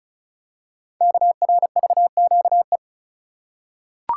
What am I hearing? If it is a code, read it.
Morse code, heard as a keyed tone, about 36 wpm, 700 Hz. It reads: KRVQE
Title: Code Groups - 5 Letters - 35wpm